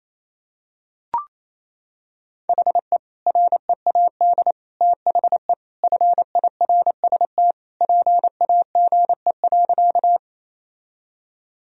Morse code, heard as a keyed tone, about 28 wpm, 700 Hz. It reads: HE READ THE FIRST PAGE.
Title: Sentences from Top 1000 Words - Review Part 1 - 28wpm